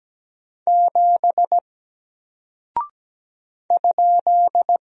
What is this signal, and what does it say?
Morse code, heard as a keyed tone, about 17 wpm, 700 Hz. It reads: 7 ?
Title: Single Character (Letter:Number:Punctuation:Prosign) 17wpm